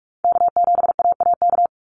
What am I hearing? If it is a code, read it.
Morse code, heard as a keyed tone, about 46 wpm, 700 Hz. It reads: K7AAK
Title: USA Call Signs 45wpm